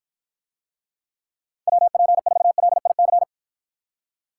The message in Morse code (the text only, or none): WP4LIC